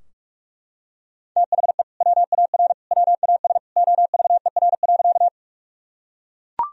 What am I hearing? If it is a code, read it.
Morse code, heard as a keyed tone, about 41 wpm, 700 Hz. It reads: THE WAR WAS OVER.
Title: Sentences from Top 1000 Words - Review Part 2 - 45wpm